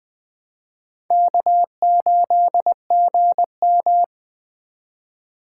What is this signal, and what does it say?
Morse code, heard as a keyed tone, about 21 wpm, 700 Hz. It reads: K8GM
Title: Speed Racing - USA 1x2 Call Signs 17wpm